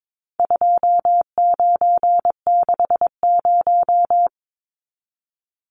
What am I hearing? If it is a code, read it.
Morse code, heard as a keyed tone, about 22 wpm, 700 Hz. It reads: 2960
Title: Numbers - 4-Digit 22wpm